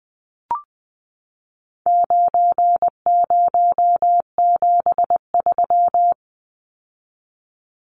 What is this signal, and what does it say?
Morse code, heard as a keyed tone, about 20 wpm, 700 Hz. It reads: 9073
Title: Numbers - 4-Digit 20wpm